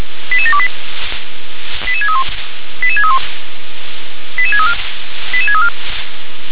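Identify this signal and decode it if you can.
Five-tone selective call (ZVEI), heard as quick runs of five tones, tones of about 70 ms. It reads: 88728 88631 88631 88634 88634
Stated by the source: Five-tone ZVEI selective call received on an SDR.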